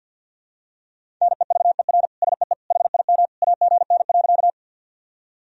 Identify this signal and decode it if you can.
Morse code, heard as a keyed tone, about 52 wpm, 700 Hz. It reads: NEVER SEE HIM AGN.
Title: Sentences from Top 300 Words and Abbreviations 50wpm